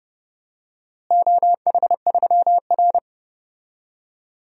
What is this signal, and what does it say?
Morse code, heard as a keyed tone, about 30 wpm, 700 Hz. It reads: OH3R
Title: DXCC-100 Call Signs - 4 Characters - Repeated 2x - 30wpm